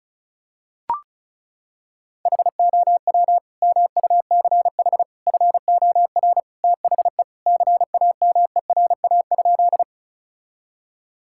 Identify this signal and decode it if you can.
Morse code, heard as a keyed tone, about 36 wpm, 700 Hz. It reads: HOW MUCH FOR THE CAMERA?
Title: Sentences from Top 1000 Words - Review Part 5 - 35wpm